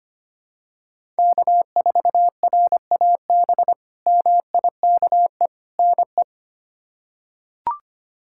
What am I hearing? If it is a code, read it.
Morse code, heard as a keyed tone, about 25 wpm, 700 Hz. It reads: K4RAB MIKE NE
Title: K1USN SST QSO Practice - 25wpm